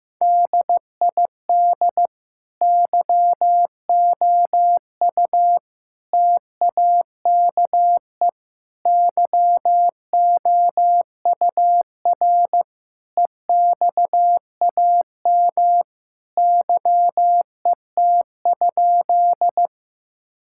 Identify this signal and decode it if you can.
Morse code, heard as a keyed tone, about 15 wpm, 700 Hz. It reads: DID YOU TAKE YOUR EXAM YET?
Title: Sentences from Top 1000 Words - Review Part 5 - 15wpm